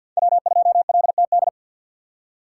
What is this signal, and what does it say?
Morse code, heard as a keyed tone, about 47 wpm, 700 Hz. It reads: W2LTD
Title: USA Call Signs - 5 Characters - 4x Character Spacing - 50wpm